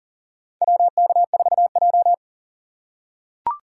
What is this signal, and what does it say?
Morse code, heard as a keyed tone, about 40 wpm, 700 Hz. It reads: WK4J
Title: USA Call Signs - 4 Characters - 6x Character Spacing - 40wpm